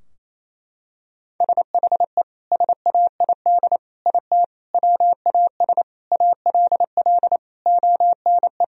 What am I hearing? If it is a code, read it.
Morse code, heard as a keyed tone, about 28 wpm, 700 Hz. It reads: SHE SAID IT WAS ALL ONE
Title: Sentences from Top 1000 Words - Review Part 2 - 28wpm